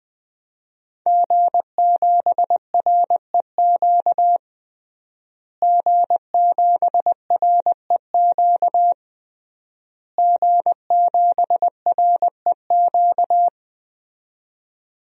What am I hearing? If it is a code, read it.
Morse code, heard as a keyed tone, about 20 wpm, 700 Hz. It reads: G7REQ G7REQ G7REQ
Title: DXCC-100 Call Signs - 5 Characters - Repeated 3x - 20wpm